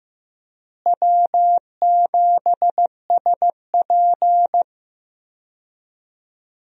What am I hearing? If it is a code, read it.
Morse code, heard as a keyed tone, about 15 wpm, 700 Hz. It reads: W7SP
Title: USA Call Signs - 4 Characters - Repeated 3x - 15wpm